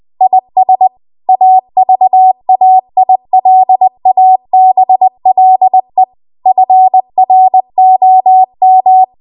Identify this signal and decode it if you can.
Morse code, heard as a keyed tone, about 20 wpm, 750 Hz. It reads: IS AVAILABLE FROM
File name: Archive / 20 WPM / 20WPM